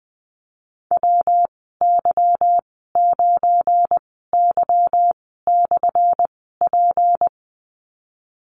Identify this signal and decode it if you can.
Morse code, heard as a keyed tone, about 20 wpm, spaced as for 15 wpm, 700 Hz. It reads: WY9Y/P
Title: 179 Lesson 29g - Slash - Call Signs 20wpm